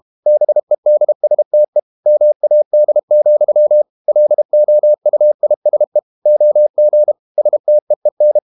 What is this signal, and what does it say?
Morse code, heard as a keyed tone, about 32 wpm, 600 Hz. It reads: BEDSTE MAD, LOUISE OG STEEN
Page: Ferie 32 wpm